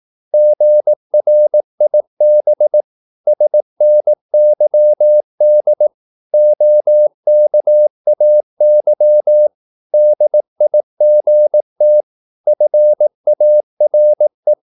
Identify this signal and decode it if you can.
Morse code, heard as a keyed tone, about 18 wpm, 600 Hz. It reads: GRIB SNYD OKAY DIGT FARE
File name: Korte ord DK 18wpm